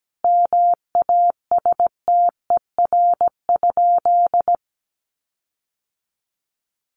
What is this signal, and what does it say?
Morse code, heard as a keyed tone, about 17 wpm, 700 Hz. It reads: MASTER?
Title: Sentences from Top 1300 Words 17wpm